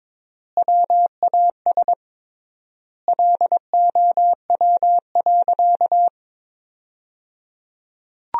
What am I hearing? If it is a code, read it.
Morse code, heard as a keyed tone, about 22 wpm, 700 Hz. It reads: WAS LOW.
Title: 3x Word-Spacing - Sentences from Top 500 Words - 22wpm